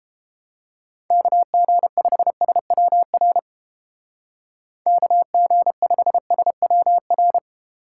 Morse code, heard as a keyed tone, about 32 wpm, 700 Hz. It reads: KG5SWR KG5SWR